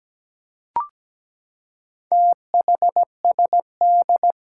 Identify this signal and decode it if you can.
Morse code, heard as a keyed tone, about 17 wpm, 700 Hz. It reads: THSD
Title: Four Challenging Letter-Number Combinations 17wpm